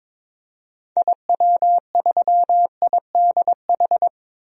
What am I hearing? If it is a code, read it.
Morse code, heard as a keyed tone, about 22 wpm, 700 Hz. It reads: IW3IDH
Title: DXCC-100 Call Signs - 6 Characters - 22wpm